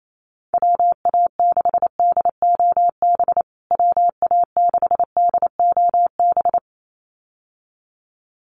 Morse code, heard as a keyed tone, about 28 wpm, 700 Hz. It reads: WA6DOB WA6DOB
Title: USA Call Signs Repeated 2x 28wpm